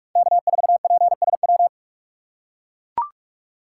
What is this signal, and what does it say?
Morse code, heard as a keyed tone, about 42 wpm, 700 Hz. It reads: K4PSW
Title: USA Call Signs - 5 Characters - 3x Character Spacing - 45wpm